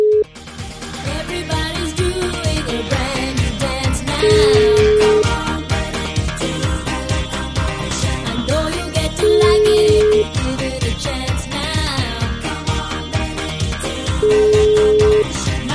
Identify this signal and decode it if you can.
Ringback tone (Europe, 425 Hz), with noise.